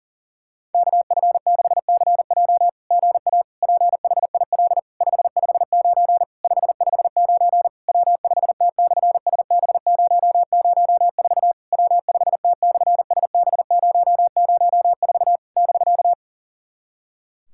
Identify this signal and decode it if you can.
Morse code, heard as a keyed tone, about 40 wpm, 690 Hz. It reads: KF6CJ GA PHIL 559 559 W5T/SB004 W5T/SB004 <BK>
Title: SOTA QSO Practice 40wpm